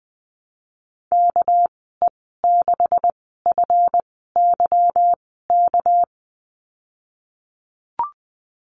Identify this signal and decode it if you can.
Morse code, heard as a keyed tone, about 20 wpm, spaced as for 15 wpm, 700 Hz. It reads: KE6FYK
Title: 219 Lesson 35g - Number 0 - Call Signs 20wpm